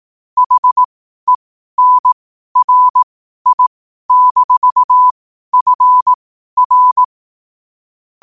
Morse code, heard as a keyed tone, about 18 wpm, spaced as for 12 wpm, 1000 Hz. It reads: HENRI-FR
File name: qotdfiles / 12wpm